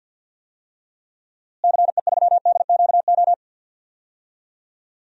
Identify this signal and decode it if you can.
Morse code, heard as a keyed tone, about 52 wpm, 700 Hz. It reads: CE3DQO
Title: DXCC-100 Call Signs - 6 Characters - Repeated 2x - 50wpm